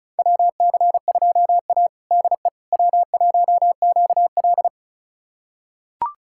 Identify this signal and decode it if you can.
Morse code, heard as a keyed tone, about 34 wpm, 700 Hz. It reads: WC2A DE W1QL
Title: USA Callsign DE Callsign - 4 Characters - 35wpm